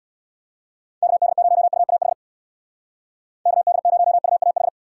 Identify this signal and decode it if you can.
Morse code, heard as a keyed tone, about 62 wpm, 700 Hz. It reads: KD0UAH KD0UAH
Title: Speed Racing - USA Call Signs 50wpm